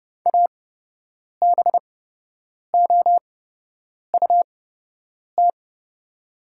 Morse code, heard as a keyed tone, about 31 wpm, spaced as for 8 wpm, 700 Hz. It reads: ABOUT
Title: 8x Character Spacing - Top 200 Words - 30wpm